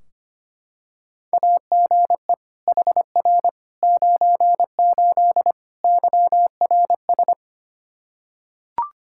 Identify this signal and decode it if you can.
Morse code, heard as a keyed tone, about 25 wpm, 700 Hz. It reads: AGE HR 98 YRS
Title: K1USN SST QSO Practice 25wpm